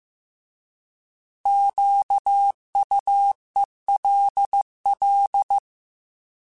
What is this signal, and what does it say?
Morse code, heard as a keyed tone, about 15 wpm, 800 Hz. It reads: QUELL
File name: Alphabetical / Alphabetical mp3s / Qs @13